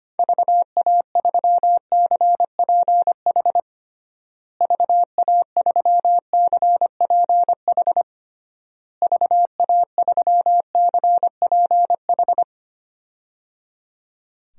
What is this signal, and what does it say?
Morse code, heard as a keyed tone, about 25 wpm, 700 Hz. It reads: VA3CPH VA3CPH VA3CPH
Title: DXCC-100 Call Signs - 6 Characters - Repeated 3x - 25wpm